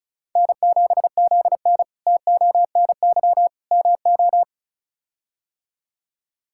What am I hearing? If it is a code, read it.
Morse code, heard as a keyed tone, about 36 wpm, 700 Hz. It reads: N7ZN TONY MO
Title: K1USN SST QSO Practice - 35wpm